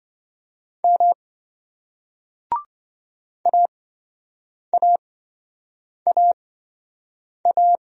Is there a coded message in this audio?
Yes: Morse code from 0.8 to 7.8 s, heard as a keyed tone, about 25 wpm, spaced as for 7 wpm, 700 Hz. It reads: M AAAA